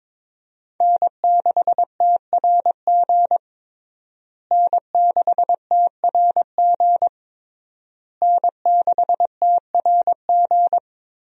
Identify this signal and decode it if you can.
Morse code, heard as a keyed tone, about 22 wpm, 700 Hz. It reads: N6TRG N6TRG N6TRG
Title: USA Call Signs - 5 Characters - Repeated 3x - 22wpm